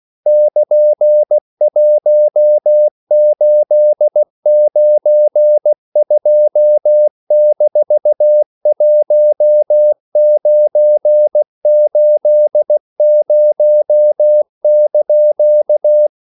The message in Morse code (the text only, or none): <KN>1892-1980)